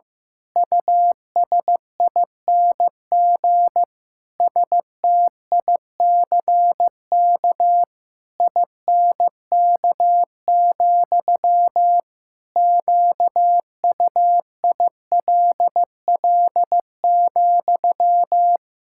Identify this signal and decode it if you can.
Morse code, heard as a keyed tone, about 15 wpm, 700 Hz. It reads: USING STICK INK, QUILL,
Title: Pangram Challenge v2 - 15wpm